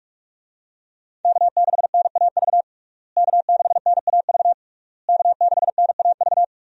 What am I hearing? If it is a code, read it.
Morse code, heard as a keyed tone, about 45 wpm, 700 Hz. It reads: K6NAV K6NAV K6NAV
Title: USA Call Signs - 5 Characters - Repeated 3x - 45wpm